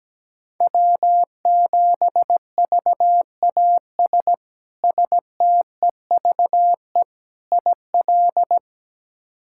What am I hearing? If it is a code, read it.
Morse code, heard as a keyed tone, about 17 wpm, 700 Hz. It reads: W7VAS STEVE IL
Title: K1USN SST QSO Practice - 17wpm